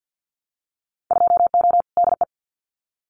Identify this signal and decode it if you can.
Morse code, heard as a keyed tone, about 50 wpm, 700 Hz. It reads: 2O DE
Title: Winter Field Day Exchanges Encore 50wpm